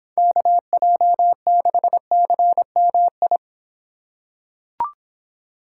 Morse code, heard as a keyed tone, about 26 wpm, 700 Hz. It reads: KJ6CMI